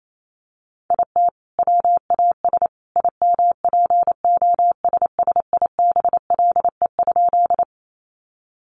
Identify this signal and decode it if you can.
Morse code, heard as a keyed tone, about 28 wpm, 700 Hz. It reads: IT WAS IMPOSSIBLE?
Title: Sentences from Top 1100 Words 28wpm